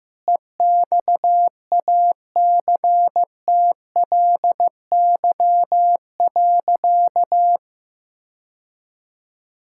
Morse code, heard as a keyed tone, about 15 wpm, 700 Hz. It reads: EXACTLY.
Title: Sentences from Top 1000 Words - Review Part 6 - 15wpm